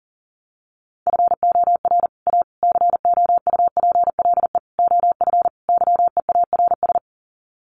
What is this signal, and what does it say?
Morse code, heard as a keyed tone, about 40 wpm, 700 Hz. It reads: FOR A COUPLE OF YEARS
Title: Sets of 5 Words Encore 40wpm